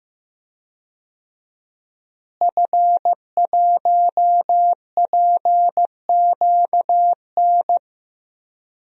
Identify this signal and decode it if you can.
Morse code, heard as a keyed tone, about 15 wpm, 700 Hz. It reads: F1PQN
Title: DXCC-100 Call Signs - 5 Characters - Repeated 3x - 15wpm